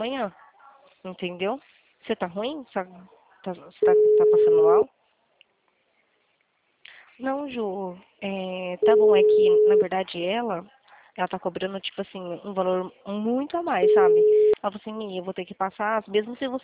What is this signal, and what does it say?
Ringback tone (Europe, 425 Hz), with speech.